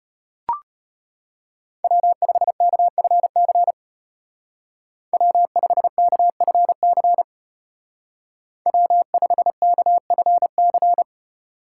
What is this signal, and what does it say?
Morse code, heard as a keyed tone, about 31 wpm, 700 Hz. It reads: W5KFC W5KFC W5KFC
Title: Speed Racing - USA 1x3 Call Signs 25wpm